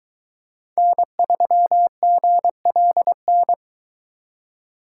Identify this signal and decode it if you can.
Morse code, heard as a keyed tone, about 22 wpm, 700 Hz. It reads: N3GLN